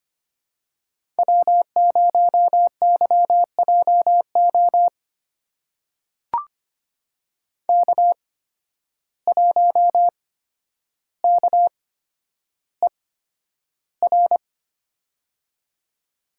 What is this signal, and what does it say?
Morse code, heard as a keyed tone, about 25 wpm, spaced as for 11 wpm, 700 Hz. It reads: W0YJO K1KER
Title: USA Call Signs - 5 Characters - 8x Character Spacing - 25wpm